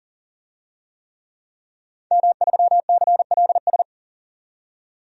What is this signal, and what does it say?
Morse code, heard as a keyed tone, about 40 wpm, 700 Hz. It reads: M3CLS
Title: DXCC-100 Call Signs - 5 Characters - 40wpm